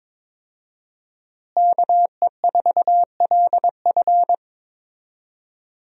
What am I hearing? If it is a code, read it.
Morse code, heard as a keyed tone, about 22 wpm, 700 Hz. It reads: KE4LF
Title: USA Call Signs - 5 Characters - Repeated 2x - 22wpm